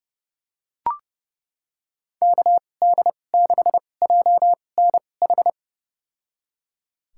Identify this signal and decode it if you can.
Morse code, heard as a keyed tone, about 30 wpm, spaced as for 22 wpm, 700 Hz. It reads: KD6JNH
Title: USA Call Signs - 6 Characters - 2x Character Spacing - 30wpm